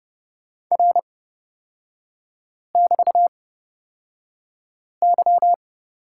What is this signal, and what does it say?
Morse code, heard as a keyed tone, about 30 wpm, spaced as for 5 wpm, 700 Hz. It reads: R<BT>Y
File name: Lesson 36e - BT - Three Characters 30wpm